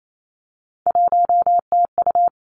Morse code, heard as a keyed tone, about 28 wpm, 700 Hz. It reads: 1TU
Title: Three Challenging Letter-Number Combinations 28wpm